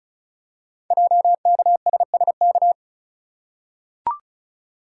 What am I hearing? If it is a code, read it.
Morse code, heard as a keyed tone, about 36 wpm, 690 Hz. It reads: JKSSK
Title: Code Groups - 5 Letter:Numbers - 35wpm